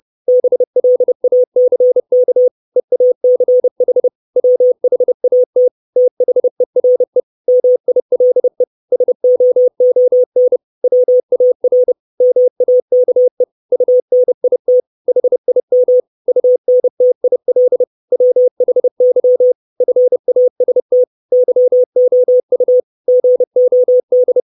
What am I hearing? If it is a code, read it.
Morse code, heard as a keyed tone, about 30 wpm, 500 Hz. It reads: BLACK EACH WHAT THERE MILE SOON WAR MAKE UNIT HIM UNTIL WHY FAST YOU GOD
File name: 632 Words 30WPM